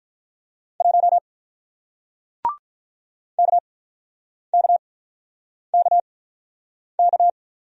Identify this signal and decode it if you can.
Morse code, heard as a keyed tone, about 48 wpm, spaced as for 9 wpm, 700 Hz. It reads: 1 KKKK